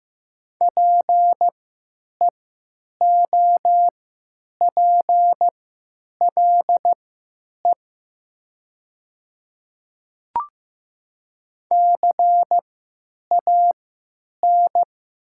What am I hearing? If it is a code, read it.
Morse code, heard as a keyed tone, about 15 wpm, spaced as for 9 wpm, 700 Hz. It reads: PEOPLE CAN
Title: 3x Character Spacing - Top 100 Words - 15wpm